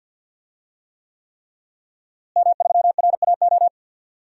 Morse code, heard as a keyed tone, about 51 wpm, 700 Hz. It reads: M3RAO
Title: DXCC-100 Call Signs - 5 Characters - Repeated 3x - 50wpm